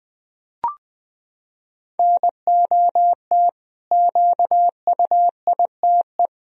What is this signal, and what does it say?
Morse code, heard as a keyed tone, about 21 wpm, 700 Hz. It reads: NOT QUITE
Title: Sentences from Top 1000 Words - Review Part 1 - 20wpm